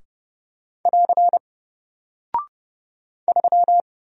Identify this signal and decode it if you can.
Morse code, heard as a keyed tone, about 30 wpm, 700 Hz. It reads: <AR> 3
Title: Single Character (Letter:Number:Punctuation:Prosign) Alt 30wpm